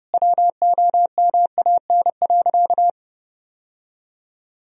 Morse code, heard as a keyed tone, about 32 wpm, 700 Hz. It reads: WOMAN.
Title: Sentences from Top 1000 Words - Review Part 3 - 30wpm